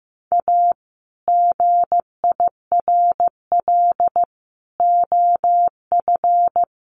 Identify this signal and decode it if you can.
Morse code, heard as a keyed tone, about 15 wpm, 700 Hz. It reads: A GIRL OF
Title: Sentences from Top 1200 Words 15wpm